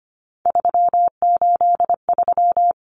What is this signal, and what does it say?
Morse code, heard as a keyed tone, about 25 wpm, 690 Hz. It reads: 383
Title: Numbers - 3-Digit 25wpm